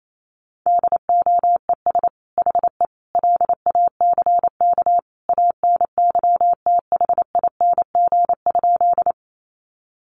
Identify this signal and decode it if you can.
Morse code, heard as a keyed tone, about 28 wpm, 700 Hz. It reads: DOES HE LACK ANYTHING?
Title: Sentences from Top 1100 Words 28wpm